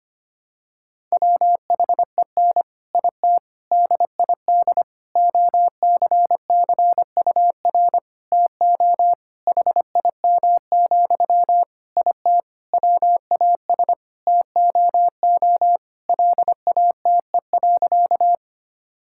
Morse code, heard as a keyed tone, about 25 wpm, 700 Hz. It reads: WHEN IT DID OCCUR TO HIM, IT WAS TOO LATE.
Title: Sentences from Top 1000 Words - Review Part 6 - 25wpm